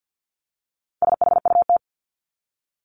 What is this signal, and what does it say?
Morse code, heard as a keyed tone, about 50 wpm, 700 Hz. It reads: SHUT
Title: Top 1751-2000 Words 50wpm